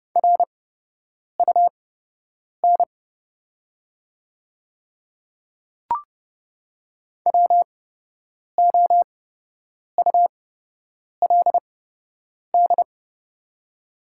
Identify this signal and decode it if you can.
Morse code, heard as a keyed tone, about 30 wpm, spaced as for 8 wpm, 700 Hz. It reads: RUN WOULD
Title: 8x Character Spacing - Top 300 Words - 30wpm